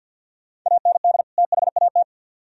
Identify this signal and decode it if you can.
Morse code, heard as a keyed tone, about 63 wpm, 700 Hz. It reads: AND THAT